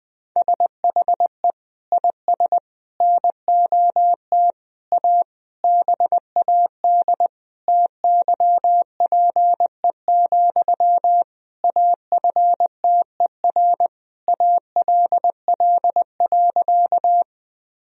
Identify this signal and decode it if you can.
Morse code, heard as a keyed tone, about 20 wpm, 700 Hz. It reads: SHE IS NOT A BAD TYPE, AFTER ALL.
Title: Sentences from Top 1000 Words - Review Part 3 - 20wpm